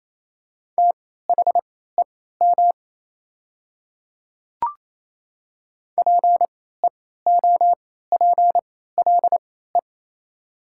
Morse code, heard as a keyed tone, about 28 wpm, spaced as for 16 wpm, 690 Hz. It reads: THEM PEOPLE
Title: 3x Character Spacing - Top 100 Words - 28wpm